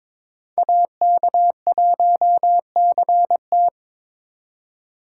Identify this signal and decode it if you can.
Morse code, heard as a keyed tone, about 26 wpm, 700 Hz. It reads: AK1CT
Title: Speed Racing - USA Call Signs 22wpm